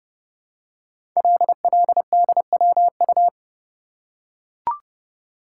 Morse code, heard as a keyed tone, about 30 wpm, 700 Hz. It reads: LLDWU
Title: Code Groups - 5 Letter:Numbers - 30wpm